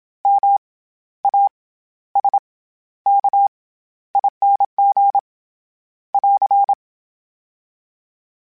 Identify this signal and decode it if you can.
Morse code, heard as a keyed tone, about 27 wpm, 800 Hz. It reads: M A S K ING <AR>
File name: Special INGs-mp3s / Special INGs-mp3s / SlowToFaster ing @25